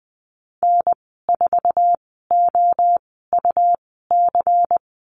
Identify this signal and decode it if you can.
Morse code, heard as a keyed tone, about 20 wpm, spaced as for 14 wpm, 700 Hz. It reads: N4OUC
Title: 128 Lesson 21g - Number 6 - Call Signs 20wpm